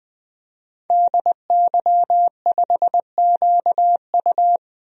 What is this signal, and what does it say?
Morse code, heard as a keyed tone, about 21 wpm, 700 Hz. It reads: DY5QU
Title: DXCC-100 Call Signs - 5 Characters - 20wpm